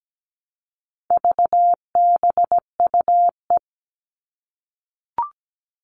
Morse code, heard as a keyed tone, about 17 wpm, 700 Hz. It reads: VBUE
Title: All Cut Numbers - 4-Digit 17wpm